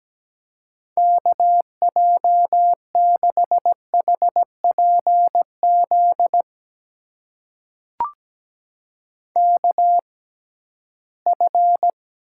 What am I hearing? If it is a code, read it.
Morse code, heard as a keyed tone, about 17 wpm, spaced as for 13 wpm, 700 Hz. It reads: KJ6HPZ KF
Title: USA Call Signs - 6 Characters - 6x Character Spacing - 17wpm